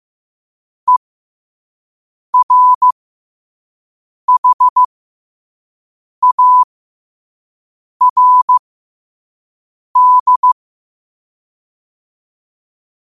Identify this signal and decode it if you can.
Morse code, heard as a keyed tone, about 15 wpm, spaced as for 5 wpm, 1000 Hz. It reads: ERHARD
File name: qotdfiles / 5wpm